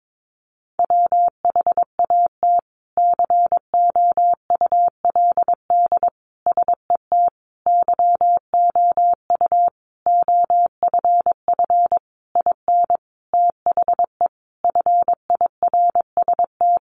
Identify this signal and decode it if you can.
Morse code, heard as a keyed tone, about 22 wpm, 700 Hz. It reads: WHAT COULD SET YOU OFF IN THE FIRST
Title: Sentences from Top 200 Words 22wpm